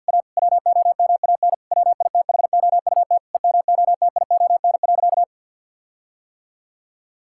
Morse code, heard as a keyed tone, about 50 wpm, 700 Hz. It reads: A WOMAN WITHOUT EMOTION.
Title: Sentences from Top 1600 Words 50wpm